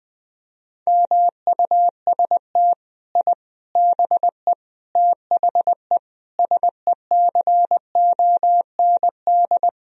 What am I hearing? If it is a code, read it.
Morse code, heard as a keyed tone, about 21 wpm, 700 Hz. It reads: MUST I BE THE SECOND
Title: Sentences from Top 1000 Words - Review Part 4 - 20wpm